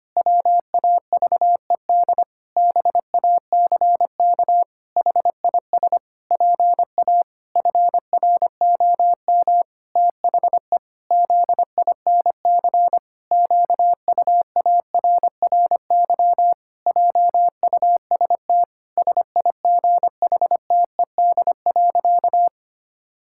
Morse code, heard as a keyed tone, about 25 wpm, 700 Hz. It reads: WAVED BACK HIS PA FROM THE ZINC QUARRY JUST SIGHTED.
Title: Pangram Challenge v2 - 25wpm